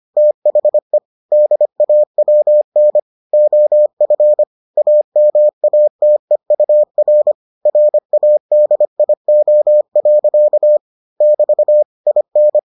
Morse code, heard as a keyed tone, about 25 wpm, 600 Hz. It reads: THE DAWN OF AMATEUR RADIO. <BT> IN